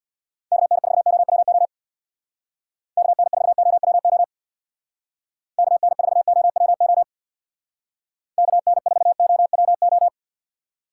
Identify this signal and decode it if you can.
Morse code, heard as a keyed tone, about 64 wpm, 700 Hz. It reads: KN4OWO KN4OWO KN4OWO KN4OWO